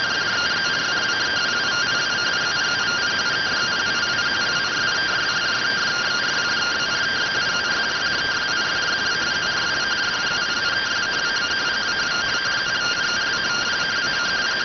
Two-tone teleprinter signal, about 170 Hz shift.